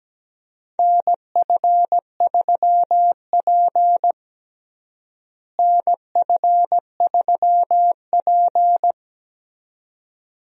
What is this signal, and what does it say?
Morse code, heard as a keyed tone, about 17 wpm, 700 Hz. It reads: NF3P NF3P
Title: USA Call Signs - 4 Characters - Repeated 2x - 17wpm